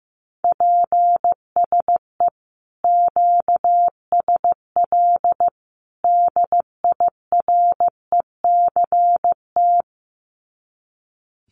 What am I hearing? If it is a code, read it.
Morse code, heard as a keyed tone, about 15 wpm, 700 Hz. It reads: PSE QSL DIRECT
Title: Common QSO Phrases 15wpm